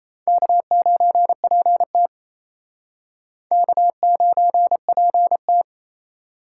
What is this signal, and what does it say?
Morse code, heard as a keyed tone, about 34 wpm, 700 Hz. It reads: K9PT K9PT
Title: Speed Racing - USA 1x2 Call Signs 28wpm